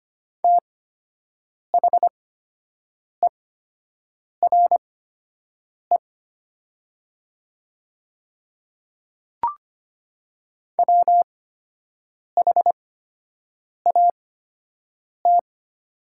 Morse code, heard as a keyed tone, about 25 wpm, spaced as for 7 wpm, 700 Hz. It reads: THERE WHAT
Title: 8x Character Spacing - Top 100 Words - 25wpm